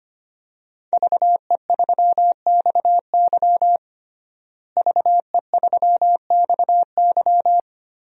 Morse code, heard as a keyed tone, about 25 wpm, 700 Hz. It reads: VE3XY VE3XY
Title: DXCC-100 Call Signs - 5 Characters - Repeated 3x - 25wpm